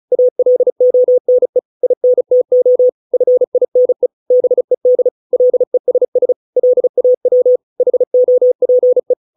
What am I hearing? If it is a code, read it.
Morse code, heard as a keyed tone, about 34 wpm, 500 Hz. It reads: ALONE INTO FINE BED LESS LAW HOPE